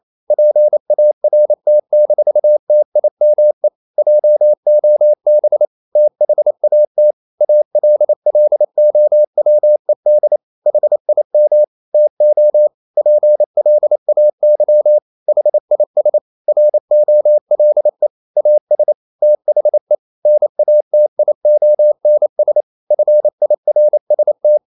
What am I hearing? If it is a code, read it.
Morse code, heard as a keyed tone, about 28 wpm, 600 Hz. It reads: PART-TIME JOB THAT ALLOWED HIM TO PLAY HIS ROLE AS THE NATIONS FIRST
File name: Never 0027wpm